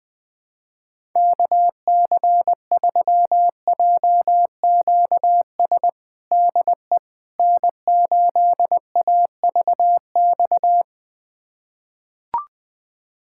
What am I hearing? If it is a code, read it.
Morse code, heard as a keyed tone, about 21 wpm, 700 Hz. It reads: KC3JQS DE N8AVX
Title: USA Callsign DE Callsign 20wpm